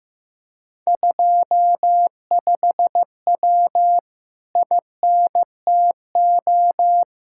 Morse code, heard as a keyed tone, about 16 wpm, 700 Hz. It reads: 25W INTO
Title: Speed-Racing Common QSO Phrases Encore - 15wpm